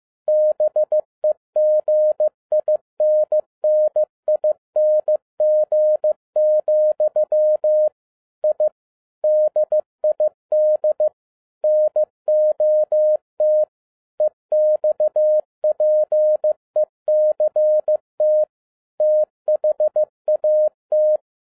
Morse code, heard as a keyed tone, about 15 wpm, 610 Hz. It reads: BEGINNING, I DID NOT EXPECT THAT
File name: 2011-07-mp3 / n42 EXCITING QRP CW OPERATION 15wpm